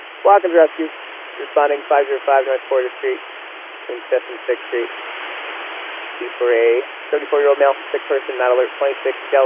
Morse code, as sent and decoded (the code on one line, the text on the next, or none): none